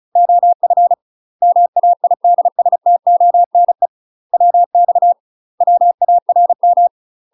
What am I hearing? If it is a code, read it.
Morse code, heard as a keyed tone, about 35 wpm, 700 Hz. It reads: OF MAIDSTONE WX WARM
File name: UK MASTER QSOs TRAINING / 35WPM / UK MASTER QSOs 0014